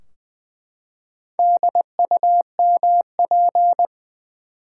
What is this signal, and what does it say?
Morse code, heard as a keyed tone, about 20 wpm, 700 Hz. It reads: DUMP
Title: Four Letter Words - 20wpm